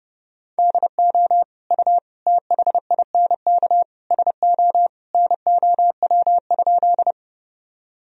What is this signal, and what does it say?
Morse code, heard as a keyed tone, about 30 wpm, 700 Hz. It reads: DO U THINK SO NOW?
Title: Sentences from Top 200 Words and Abbreviations 30wpm